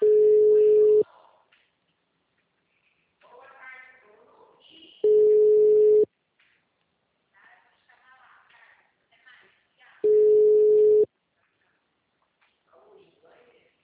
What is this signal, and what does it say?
Ringback tone (Europe, 425 Hz), with speech.
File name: out- -2016- - - .87858